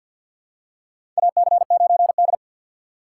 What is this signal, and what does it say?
Morse code, heard as a keyed tone, about 53 wpm, 700 Hz. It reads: AC9D
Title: USA Call Signs - 4 Characters - 6x Character Spacing - 50wpm